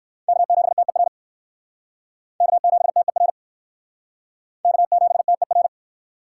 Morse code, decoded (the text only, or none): K7TER K7TER K7TER